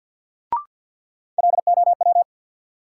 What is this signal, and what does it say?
Morse code, heard as a keyed tone, about 53 wpm, 700 Hz. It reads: LOW
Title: Sentences from Top 500 Words - New Word Drill - 50wpm